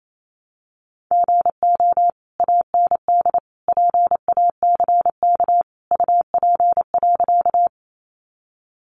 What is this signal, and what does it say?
Morse code, heard as a keyed tone, about 28 wpm, 700 Hz. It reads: GO AND PACK UP.
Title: Sentences from Top 1300 Words 28wpm